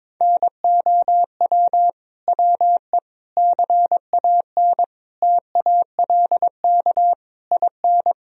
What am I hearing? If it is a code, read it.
Morse code, heard as a keyed tone, about 22 wpm, 700 Hz. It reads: NOW WE CAN TALK IN
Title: Sentences from Top 1000 Words - Review Part 4 - 22wpm